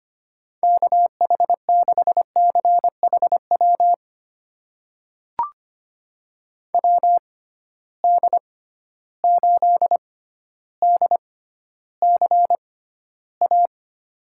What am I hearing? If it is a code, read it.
Morse code, heard as a keyed tone, about 25 wpm, spaced as for 13 wpm, 700 Hz. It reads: KH6CHW WD8DCA
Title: USA Call Signs - 6 Characters - 6x Character Spacing - 25wpm